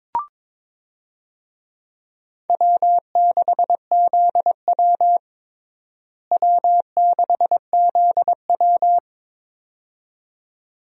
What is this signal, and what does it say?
Morse code, heard as a keyed tone, about 22 wpm, 700 Hz. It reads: W6ZW W6ZW
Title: USA Call Signs - 4 Characters - Repeated 2x - 22wpm